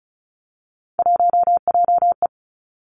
Morse code, heard as a keyed tone, about 36 wpm, 700 Hz. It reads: 1JE
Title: Three Challenging Letter-Number Combinations 35wpm